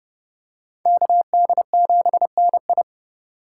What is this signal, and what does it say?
Morse code, heard as a keyed tone, about 30 wpm, 700 Hz. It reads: KD7NI